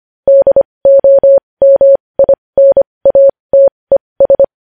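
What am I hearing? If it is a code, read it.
Morse code, heard as a keyed tone, about 25 wpm, spaced as for 20 wpm, 550 Hz. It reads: DOMINATES
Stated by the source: Morse code news headlines practice at 20 wpm.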